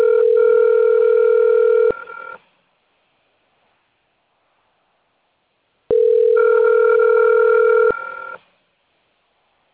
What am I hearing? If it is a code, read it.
Ringback tone (North America), 440 + 480 Hz.
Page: Nova Odessa 1